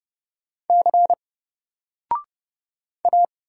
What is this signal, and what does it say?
Morse code, heard as a keyed tone, about 30 wpm, 700 Hz. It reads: C A
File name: Single-Letter-Number-30wpm